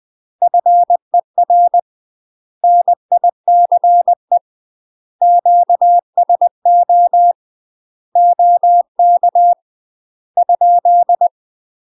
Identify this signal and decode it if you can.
Morse code, heard as a keyed tone, about 20 wpm, 700 Hz. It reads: FER NICE QSO OK ?
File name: UK MASTER QSOs TRAINING/20WPM